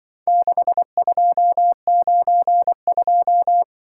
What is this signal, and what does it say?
Morse code, heard as a keyed tone, about 23 wpm, 700 Hz. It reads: 6292
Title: Speed-Racing - Four Letter-Number Combinations - 20wpm